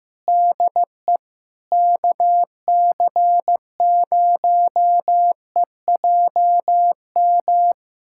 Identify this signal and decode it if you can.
Morse code, heard as a keyed tone, about 15 wpm, 700 Hz. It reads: DE KC0EJM
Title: USA Callsign DE Callsign 15wpm